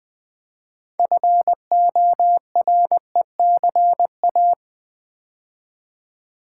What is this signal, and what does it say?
Morse code, heard as a keyed tone, about 20 wpm, 700 Hz. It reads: FORECA
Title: Build A Word - 3 Letters At A Time - 8-Letter Words - 20wpm